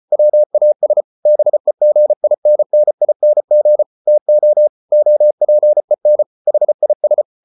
Morse code, heard as a keyed tone, about 34 wpm, 600 Hz. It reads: WAS BEGINNING TO OPEN HIS